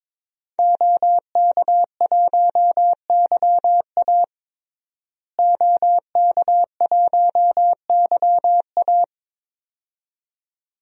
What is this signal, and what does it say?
Morse code, heard as a keyed tone, about 22 wpm, 700 Hz. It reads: OK1YA OK1YA
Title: DXCC-100 Call Signs - 5 Characters - Repeated 2x - 22wpm